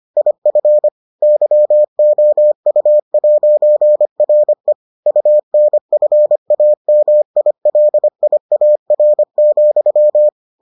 Morse code, heard as a keyed tone, about 25 wpm, 600 Hz. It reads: IF YOU'RE UNFAMILIAR,